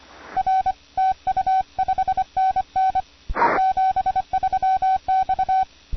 Morse code, heard as a keyed tone, about 25 wpm, 760 Hz. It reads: R TU 5NN 73 X